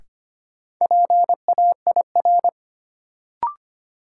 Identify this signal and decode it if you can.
Morse code, heard as a keyed tone, about 25 wpm, 700 Hz. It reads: PAIR
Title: Sets of 3 Words - New Word Drill - 25wpm